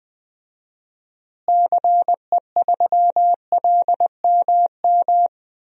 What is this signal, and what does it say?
Morse code, heard as a keyed tone, about 20 wpm, 700 Hz. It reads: CE3LMM
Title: DXCC-100 Call Signs - 6 Characters - Repeated 3x - 20wpm